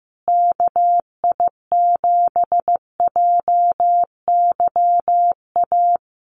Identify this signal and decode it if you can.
Morse code, heard as a keyed tone, about 15 wpm, 700 Hz. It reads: KI7JYA
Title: USA Call Signs Repeated 3x 15wpm